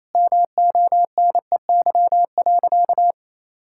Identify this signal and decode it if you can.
Morse code, heard as a keyed tone, about 28 wpm, 700 Hz. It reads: MONEY.
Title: Sentences from Top 1000 Words - Review Part 6 - 28wpm